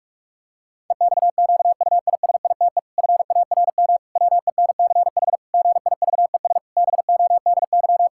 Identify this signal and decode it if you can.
Morse code, heard as a keyed tone, about 46 wpm, 700 Hz. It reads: EXQUISITE FARM WENCH GIVES BODY
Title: Pangram Challenge v2 - 45wpm